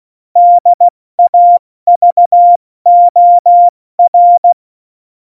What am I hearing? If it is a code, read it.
Morse code, heard as a keyed tone, about 16 wpm, spaced as for 14 wpm, 700 Hz. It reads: DAVOR
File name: Morsekurs / Witze / Witz 16. Januar 2018, 70 F80